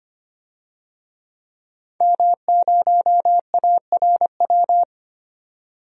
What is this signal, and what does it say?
Morse code, heard as a keyed tone, about 25 wpm, 700 Hz. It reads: M0ARW
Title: DXCC-100 Call Signs - 5 Characters - 25wpm